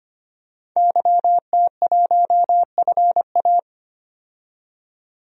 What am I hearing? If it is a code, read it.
Morse code, heard as a keyed tone, about 25 wpm, 700 Hz. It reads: YT1FA
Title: DXCC-100 Call Signs - 5 Characters - 25wpm